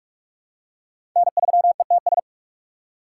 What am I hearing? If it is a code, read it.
Morse code, heard as a keyed tone, about 46 wpm, 700 Hz. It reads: N3ETS
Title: USA Call Signs - 5 Characters - 3x Character Spacing - 45wpm